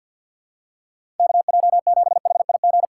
Morse code, heard as a keyed tone, about 53 wpm, 700 Hz. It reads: KJ7HIG
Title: USA Call Signs - 6 Characters - 3x Character Spacing - 50wpm